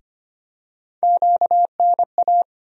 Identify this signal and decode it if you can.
Morse code, heard as a keyed tone, about 25 wpm, 700 Hz. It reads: QNA
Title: USA Call Sign Postfixes - 3 Letters 25wpm